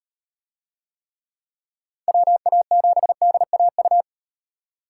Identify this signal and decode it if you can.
Morse code, heard as a keyed tone, about 38 wpm, 700 Hz. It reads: WA7DAU